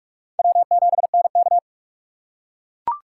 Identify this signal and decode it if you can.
Morse code, heard as a keyed tone, about 40 wpm, 700 Hz. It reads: W7NK